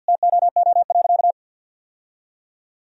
Morse code, heard as a keyed tone, about 50 wpm, 700 Hz. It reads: TOO.
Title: Sentences from Top 1000 Words - Review Part 1 - 50wpm